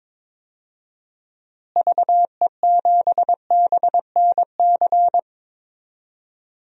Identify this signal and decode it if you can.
Morse code, heard as a keyed tone, about 22 wpm, 700 Hz. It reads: VE7BNC
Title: DXCC-100 Call Signs - 6 Characters - Repeated 2x - 22wpm